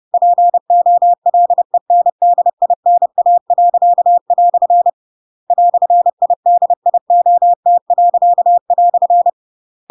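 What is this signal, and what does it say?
Morse code, heard as a keyed tone, about 30 wpm, 700 Hz. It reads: POLENDINA." "IDIOT."
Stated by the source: Chapter 2 of The Adventures of Pinocchio read as a book in Morse code (CW) at 30 wpm, by ik1hge.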